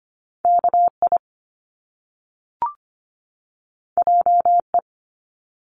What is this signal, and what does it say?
Morse code, heard as a keyed tone, about 25 wpm, 700 Hz. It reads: KI JE
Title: Top Two Letter Combinations in Words 25wpm